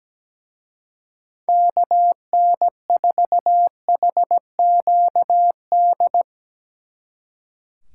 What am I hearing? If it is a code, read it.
Morse code, heard as a keyed tone, about 18 wpm, 700 Hz. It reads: KN4HQD
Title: Speed Racing - USA 2x3 Call Signs 17wpm